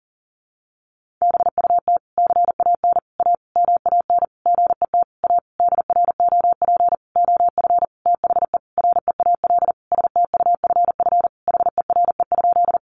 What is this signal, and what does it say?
Morse code, heard as a keyed tone, about 40 wpm, 700 Hz. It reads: BUT CAN A MAN GET A DROP OF THE REAL STUFF HERE?
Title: Sentences from Top 1400 Words 40wpm